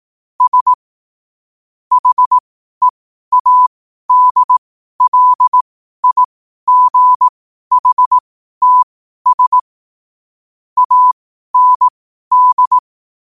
Morse code, heard as a keyed tone, about 18 wpm, spaced as for 12 wpm, 1000 Hz. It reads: S HEADLIGHTS AND
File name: qotdfiles / 12wpm